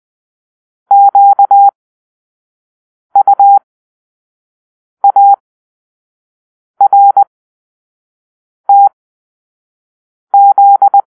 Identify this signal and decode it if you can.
Morse code, heard as a keyed tone, about 20 wpm, spaced as for 5 wpm, 800 Hz. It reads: QUARTZ
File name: arrl / Grp-5 Drill-4